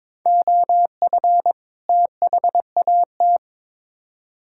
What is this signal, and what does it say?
Morse code, heard as a keyed tone, about 22 wpm, 700 Hz. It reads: OF THAT
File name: Build A Phrase - 2-Word Phrases - 22wpm